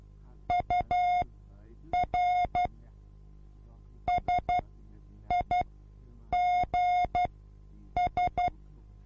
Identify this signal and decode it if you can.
Morse code, heard as a keyed tone, about 12 wpm, spaced as for 8 wpm, 720 Hz. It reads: UR SIGS
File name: repos / Morsekurs / DJ4UF Morsen24